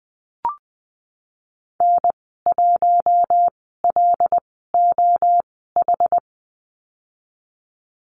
Morse code, heard as a keyed tone, about 20 wpm, spaced as for 14 wpm, 700 Hz. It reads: N1LOH
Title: 088 Lesson 15g - Number 5 - Call Signs 20wpm